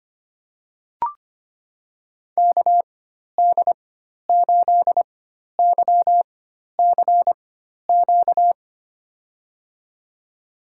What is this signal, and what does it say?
Morse code, heard as a keyed tone, about 25 wpm, spaced as for 12 wpm, 700 Hz. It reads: KD8YCQ
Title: USA Call Signs - 6 Characters - 4x Character Spacing - 25wpm